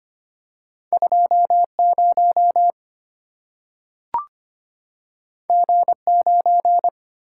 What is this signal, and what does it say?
Morse code, heard as a keyed tone, about 25 wpm, 700 Hz. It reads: 20 G9
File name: Two Letter-Number Combinations - 25wpm